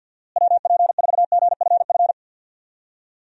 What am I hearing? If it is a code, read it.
Morse code, heard as a keyed tone, about 46 wpm, 700 Hz. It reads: WP4GFF
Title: USA Call Signs - 6 Characters - Repeated 3x - 50wpm